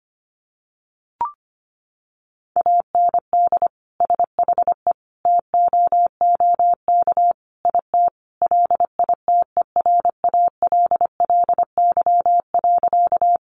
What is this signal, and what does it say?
Morse code, heard as a keyed tone, about 25 wpm, 700 Hz. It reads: AND SHE TOOK IT LITERALLY.
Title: Sentences from Top 1400 Words 25wpm